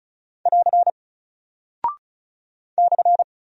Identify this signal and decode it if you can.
Morse code, heard as a keyed tone, about 36 wpm, 700 Hz. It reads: <AR> /
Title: Single Character (Letter:Number:Punctuation:Prosign) 35wpm